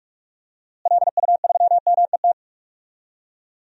Morse code, heard as a keyed tone, about 46 wpm, 700 Hz. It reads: LU3MET
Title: DXCC-100 Call Signs - 6 Characters - Repeated 3x - 45wpm